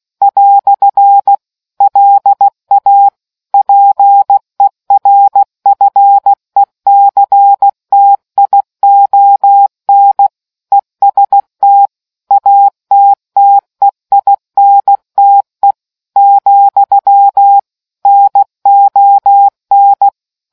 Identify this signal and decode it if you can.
Morse code, heard as a keyed tone, about 16 wpm, 800 Hz. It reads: " LA PERFECTION EST ATTEINTE , NON